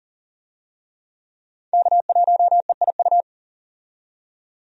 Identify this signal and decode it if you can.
Morse code, heard as a keyed tone, about 40 wpm, 700 Hz. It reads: K1EIU
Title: USA Call Signs - 5 Characters - 40wpm